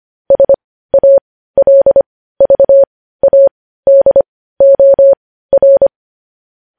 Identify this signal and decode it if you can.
Morse code, heard as a keyed tone, about 25 wpm, spaced as for 15 wpm, 550 Hz. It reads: SALVADOR
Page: news-headlines-in-morse-code-at-15-wpm